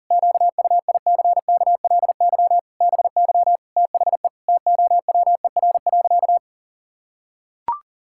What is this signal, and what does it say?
Morse code, heard as a keyed tone, about 40 wpm, 700 Hz. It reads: QUICKLY BY THE TOWER.
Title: Pangram Challenge v2 - 40wpm